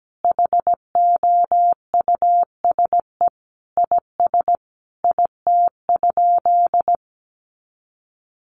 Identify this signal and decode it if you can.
Morse code, heard as a keyed tone, about 17 wpm, 700 Hz. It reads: HOUSE IS IT?
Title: Sentences from Top 200 Words 17wpm